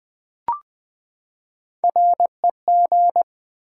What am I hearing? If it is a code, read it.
Morse code, heard as a keyed tone, about 20 wpm, 700 Hz. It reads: REG
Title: Build A Word - 3 Letters At A Time - 8-Letter Words - 20wpm